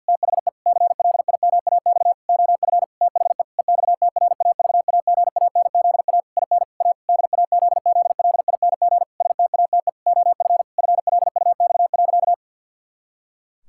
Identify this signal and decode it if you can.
Morse code, heard as a keyed tone, about 52 wpm, 700 Hz. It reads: THE CLIMAX OF THE EXTRAVAGANZA IN A DAZZLING STATE OF FLUX.
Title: Pangram Challenge v2 - 50wpm